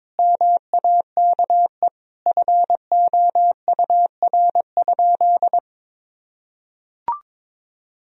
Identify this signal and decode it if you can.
Morse code, heard as a keyed tone, about 22 wpm, 700 Hz. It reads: MAKE FOUR?
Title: Sentences from Top 1000 Words - Review Part 1 - 22wpm